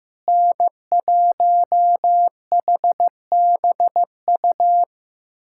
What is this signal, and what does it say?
Morse code, heard as a keyed tone, about 15 wpm, 700 Hz. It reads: N1HBU
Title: USA Callsign DE Callsign - 5 Characters - 15wpm